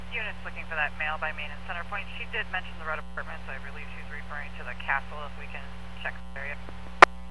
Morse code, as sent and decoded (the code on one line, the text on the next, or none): none